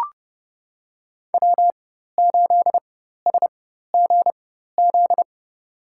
Morse code, heard as a keyed tone, about 29 wpm, spaced as for 14 wpm, 700 Hz. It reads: W8SGZ
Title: USA Call Signs - 5 Characters - 4x Character Spacing - 30wpm